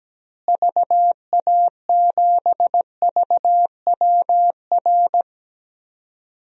Morse code, heard as a keyed tone, about 17 wpm, 700 Hz. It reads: VA7VWR
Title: DXCC-100 Call Signs - 6 Characters - Repeated 2x - 17wpm